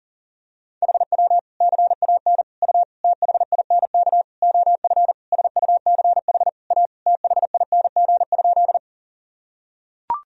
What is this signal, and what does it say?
Morse code, heard as a keyed tone, about 40 wpm, 700 Hz. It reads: HW CAN U THINK OF SUCH A THING?
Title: Sentences from Top 200 Words and Abbreviations 40wpm